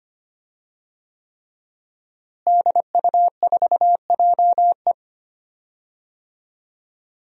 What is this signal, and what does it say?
Morse code, heard as a keyed tone, about 25 wpm, 700 Hz. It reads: DU4JE
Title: DXCC-100 Call Signs - 5 Characters - Repeated 3x - 25wpm